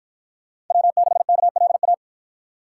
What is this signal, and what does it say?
Morse code, heard as a keyed tone, about 53 wpm, 700 Hz. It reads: W6KLA